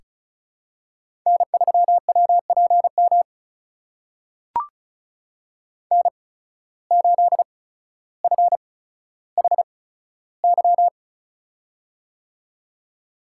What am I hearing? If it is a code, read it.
Morse code, heard as a keyed tone, about 36 wpm, spaced as for 15 wpm, 700 Hz. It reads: N3WPM N8FHY
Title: USA Call Signs - 5 Characters - 8x Character Spacing - 35wpm